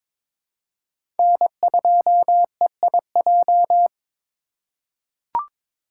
Morse code, heard as a keyed tone, about 22 wpm, 700 Hz. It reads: N2EIJ
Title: USA Call Signs - 5 Characters - 4x Character Spacing - 22wpm